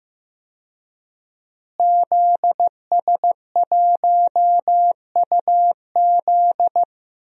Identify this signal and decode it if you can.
Morse code, heard as a keyed tone, about 15 wpm, 700 Hz. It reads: ZS1UZ
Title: DXCC-100 Call Signs - 5 Characters - Repeated 2x - 15wpm